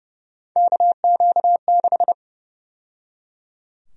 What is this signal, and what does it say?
Morse code, heard as a keyed tone, about 30 wpm, 700 Hz. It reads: KQ6
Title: USA Call Sign Prefixes - 30wpm